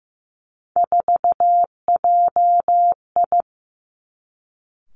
Morse code, heard as a keyed tone, about 15 wpm, 700 Hz. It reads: 4JI
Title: Three Challenging Letter-Number Combinations 15wpm